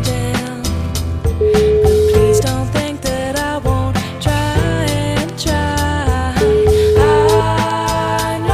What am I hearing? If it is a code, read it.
Ringback tone (Europe, 425 Hz), with noise.